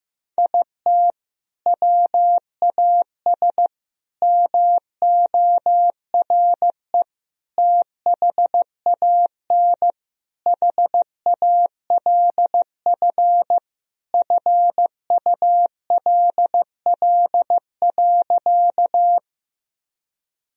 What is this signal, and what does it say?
Morse code, heard as a keyed tone, about 15 wpm, 700 Hz. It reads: IT WAS MORE THAN HALF FULL.
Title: Sentences from Top 1000 Words - Review Part 3 - 15wpm